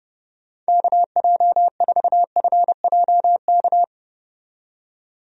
Morse code, heard as a keyed tone, about 30 wpm, 700 Hz. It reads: KJ4FJK
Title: USA Call Signs - 6 Characters - 30wpm